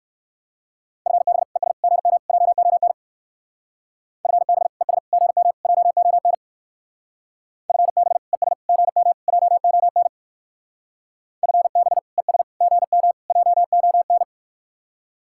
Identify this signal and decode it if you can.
Morse code, heard as a keyed tone, about 63 wpm, 700 Hz. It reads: FB ES GM JON FB ES GM JON FB ES GM JON FB ES GM JON